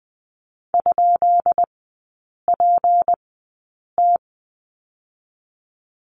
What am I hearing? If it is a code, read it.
Morse code, heard as a keyed tone, about 20 wpm, spaced as for 8 wpm, 690 Hz. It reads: ?PT
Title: 172 Lesson 28e - Number 9 - Three Characters 20wpm